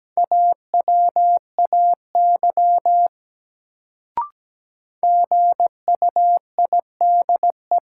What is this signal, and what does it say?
Morse code, heard as a keyed tone, about 17 wpm, 700 Hz. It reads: AWAY GUIDE
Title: Sentences from Top 600 Words - New Word Drill - 17wpm